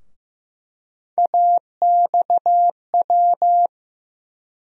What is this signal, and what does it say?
Morse code, heard as a keyed tone, about 15 wpm, 700 Hz. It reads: AXW
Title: USA Call Sign Postfixes - 3 Letters 15wpm